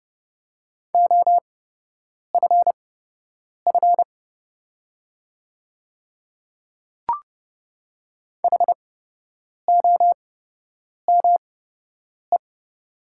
Morse code, heard as a keyed tone, about 29 wpm, spaced as for 8 wpm, 690 Hz. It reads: OFF HOME
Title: 8x Character Spacing - Top 200 Words - 30wpm